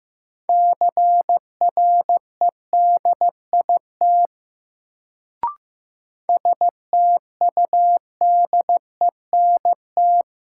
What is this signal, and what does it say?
Morse code, heard as a keyed tone, about 15 wpm, 700 Hz. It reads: CREDIT STUDENT
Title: Sentences from Top 1000 Words - New Word Drill - 15wpm